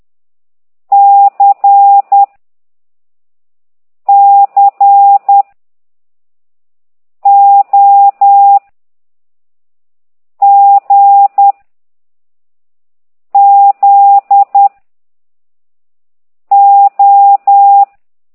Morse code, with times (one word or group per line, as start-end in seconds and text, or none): 0.9-2.2 C
4.1-5.4 C
7.3-8.6 O
10.4-11.5 G
13.3-14.7 Z
16.5-17.8 O